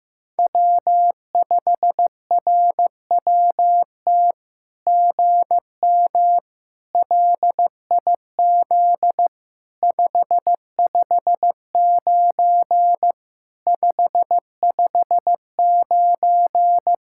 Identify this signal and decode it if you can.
Morse code, heard as a keyed tone, about 14 wpm, 700 Hz. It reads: W5RWT GM LIZ 559 559
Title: SOTA QSO Practice 15wpm